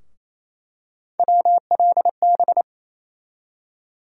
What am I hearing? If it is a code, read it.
Morse code, heard as a keyed tone, about 28 wpm, 700 Hz. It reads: WLB
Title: USA Call Sign Postfixes - 3 Letters 28wpm